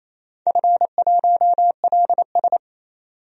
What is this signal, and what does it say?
Morse code, heard as a keyed tone, about 28 wpm, 700 Hz. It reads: F1LS
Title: DXCC-100 Call Signs - 4 Characters - 28wpm